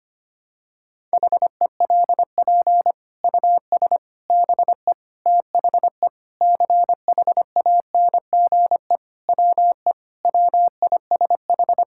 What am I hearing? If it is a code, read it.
Morse code, heard as a keyed tone, about 25 wpm, 700 Hz. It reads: HELP US BE THE CHANGE WE WISH
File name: Sets of 9 Words - 25wpm